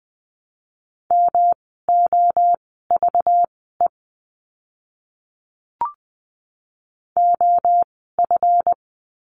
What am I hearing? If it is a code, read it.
Morse code, heard as a keyed tone, about 20 wpm, spaced as for 14 wpm, 700 Hz. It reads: MOVE OF
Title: 237 Lesson 38f - Letter q - Words 20wpm